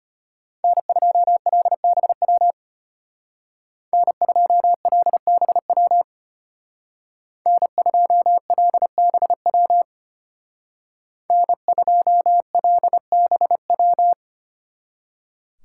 Morse code, heard as a keyed tone, about 33 wpm, 700 Hz. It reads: N2LBW N2LBW N2LBW N2LBW